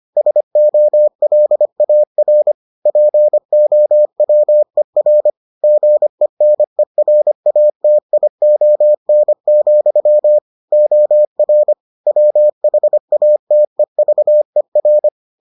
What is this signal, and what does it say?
Morse code, heard as a keyed tone, about 25 wpm, 600 Hz. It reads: SOLAR POWER GENERATION, OR WHATEVER